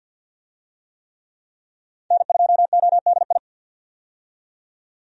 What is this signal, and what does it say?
Morse code, heard as a keyed tone, about 48 wpm, 700 Hz. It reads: N2ODI
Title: USA Call Signs - 5 Characters - 50wpm